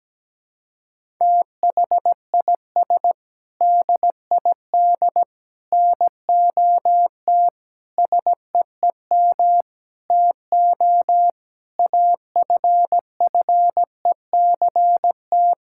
Morse code, heard as a keyed tone, about 17 wpm, 700 Hz. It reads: THIS DID NOT SEEM TO AFFECT
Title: Sentences from Top 1000 Words - Review Part 6 - 17wpm